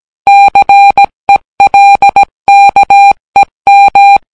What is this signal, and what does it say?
Morse code, heard as a keyed tone, about 17 wpm, 800 Hz. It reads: CELKEM